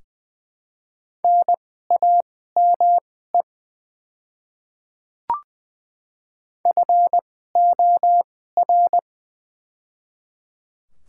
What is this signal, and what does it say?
Morse code, heard as a keyed tone, about 20 wpm, spaced as for 14 wpm, 700 Hz. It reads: NAME FOR
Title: 2x Character Spacing - Top 200 Words - 20wpm